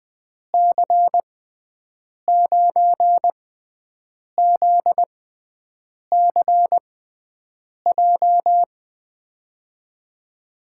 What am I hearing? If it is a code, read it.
Morse code, heard as a keyed tone, about 20 wpm, spaced as for 7 wpm, 700 Hz. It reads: C9ZCJ
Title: USA Call Signs - 6 Characters - 6x Character Spacing - 20wpm